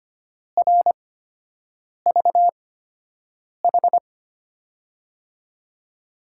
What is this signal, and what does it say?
Morse code, heard as a keyed tone, about 25 wpm, spaced as for 7 wpm, 700 Hz. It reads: RVH
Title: USA Call Signs - 5 Characters - 8x Character Spacing - 25wpm